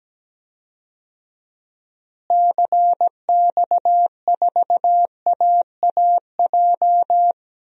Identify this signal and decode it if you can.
Morse code, heard as a keyed tone, about 17 wpm, 700 Hz. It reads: CX4AAJ
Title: DXCC-100 Call Signs - 6 Characters - Repeated 3x - 17wpm